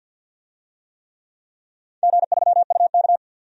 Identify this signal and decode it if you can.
Morse code, heard as a keyed tone, about 53 wpm, 700 Hz. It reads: G3UK
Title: DXCC-100 Call Signs - 4 Characters - 50wpm